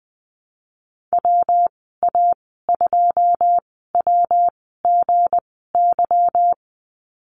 Morse code, heard as a keyed tone, about 20 wpm, spaced as for 14 wpm, 700 Hz. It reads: WA2WGY
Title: 179 Lesson 29g - Slash - Call Signs 20wpm